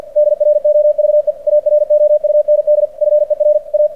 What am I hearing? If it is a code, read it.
Morse code, heard as a keyed tone, about 48 wpm, 600 Hz. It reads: DROVE AROUND FER A
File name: archived / CRSnet